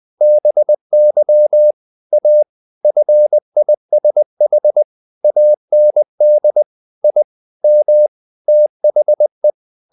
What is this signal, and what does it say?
Morse code, heard as a keyed tone, about 20 wpm, 600 Hz. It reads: BY A FISH AND I M THE
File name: the old man and the sea / the old man and the sea part 1 20wpm